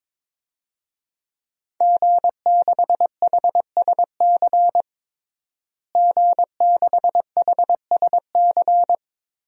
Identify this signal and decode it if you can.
Morse code, heard as a keyed tone, about 22 wpm, 700 Hz. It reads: G6HSC G6HSC
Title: DXCC-100 Call Signs - 5 Characters - Repeated 2x - 22wpm